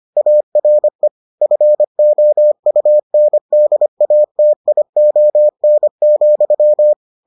Morse code, heard as a keyed tone, about 25 wpm, 600 Hz. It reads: ARE FOUNDATION,